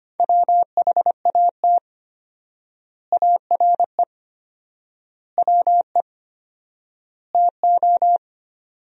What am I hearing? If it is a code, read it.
Morse code, heard as a keyed tone, about 25 wpm, 700 Hz. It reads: WHAT ARE WE TO
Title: 4x Word-Spacing - Sentences from Top 100 Words - 25wpm